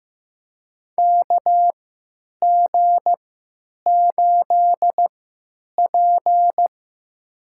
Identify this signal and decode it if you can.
Morse code, heard as a keyed tone, about 16 wpm, spaced as for 9 wpm, 700 Hz. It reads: KG8P
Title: USA Call Signs - 3x Character Spacing - 15wpm